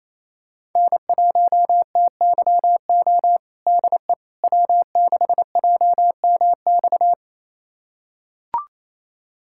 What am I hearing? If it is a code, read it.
Morse code, heard as a keyed tone, about 27 wpm, 700 Hz. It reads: N1TYO DE W6JMX
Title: USA Callsign DE Callsign - 5 Characters - 28wpm